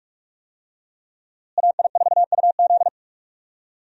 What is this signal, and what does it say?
Morse code, heard as a keyed tone, about 46 wpm, 700 Hz. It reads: AI4UZ
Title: USA Call Signs - 5 Characters - Repeated 3x - 45wpm